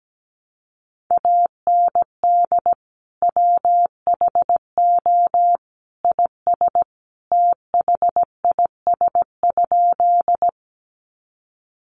Morse code, heard as a keyed tone, about 17 wpm, 700 Hz. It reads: AND WHO IS THIS?
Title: Sentences from Top 100 Words 17wpm